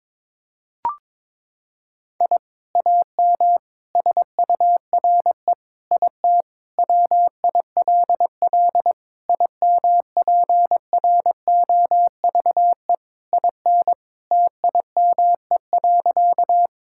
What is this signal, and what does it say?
Morse code, heard as a keyed tone, about 22 wpm, 700 Hz. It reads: I AM SURE IT WILL IMPROVE IN TIME.
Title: Sentences from Top 1000 Words - Review Part 5 - 22wpm